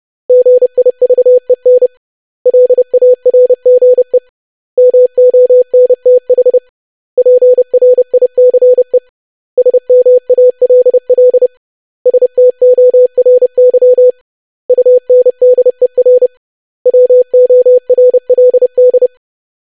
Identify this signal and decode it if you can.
Morse code, heard as a keyed tone, about 30 wpm, 500 Hz. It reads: GIVEN LARGE MONTH PRICE SMALL STORY UNDER WORLD
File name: lesson / web / basic500-25wpm